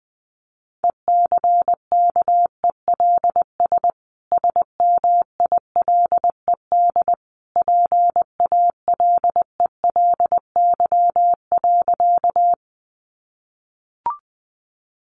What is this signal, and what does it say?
Morse code, heard as a keyed tone, about 20 wpm, 700 Hz. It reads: ECKELS SMILED PALELY.
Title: A Sound Of Thunder 20wpm